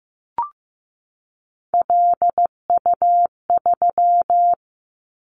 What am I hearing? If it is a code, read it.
Morse code, heard as a keyed tone, about 15 wpm, 700 Hz. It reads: LU3
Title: Callsign Prefixes for Top 50 DXCC - 15wpm